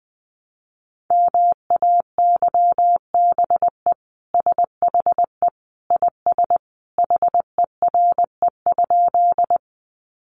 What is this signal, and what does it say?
Morse code, heard as a keyed tone, about 20 wpm, 700 Hz. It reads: MAYBE SHE IS HERE?
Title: Sentences from Top 1000 Words 20wpm